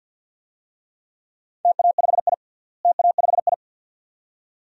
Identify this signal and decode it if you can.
Morse code, heard as a keyed tone, about 46 wpm, 700 Hz. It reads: TA5I TA5I
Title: DXCC-100 Call Signs - 4 Characters - Repeated 2x - 50wpm